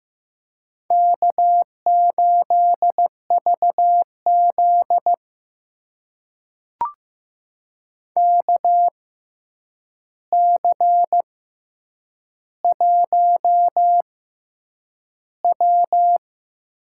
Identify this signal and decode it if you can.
Morse code, heard as a keyed tone, about 15 wpm, spaced as for 8 wpm, 700 Hz. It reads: K8VZ KC1W
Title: USA Call Signs - 4 Characters - 6x Character Spacing - 15wpm